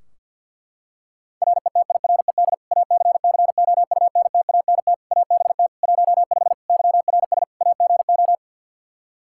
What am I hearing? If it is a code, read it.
Morse code, heard as a keyed tone, about 47 wpm, 700 Hz. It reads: RETIRED ACCOUNTANT ABT 15 YRS AGO
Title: Practice QSO - Exchange 3 - More Info - 50wpm